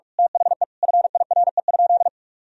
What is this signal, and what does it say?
Morse code, heard as a keyed tone, about 41 wpm, 700 Hz. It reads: THE FIRE?
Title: Sentences from Top 1000 Words - Review Part 3 - 45wpm